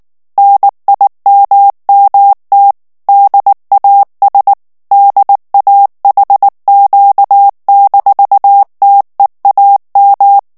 Morse code, heard as a keyed tone, about 19 wpm, 800 Hz. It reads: NIMMT DAS DAHQ-TEAM
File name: morse mp3 / ALLEMAND texte clair 2 vitesse 18 mots minute